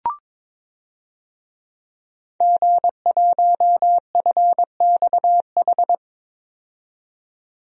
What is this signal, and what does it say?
Morse code, heard as a keyed tone, about 22 wpm, 700 Hz. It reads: G1FXH
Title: DXCC-100 Call Signs - 5 Characters - 22wpm